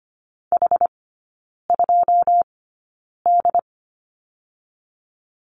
Morse code, heard as a keyed tone, about 25 wpm, spaced as for 9 wpm, 700 Hz. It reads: H2D
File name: Lesson 14e - Number 2 - Three Characters 25wpm